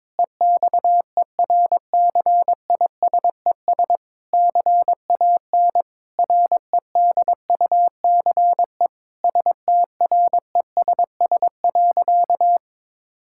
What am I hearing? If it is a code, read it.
Morse code, heard as a keyed tone, about 22 wpm, 700 Hz. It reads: EXERCISES CAN REDUCE STRESS.
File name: Difficult Sentences with Hints - 22wpm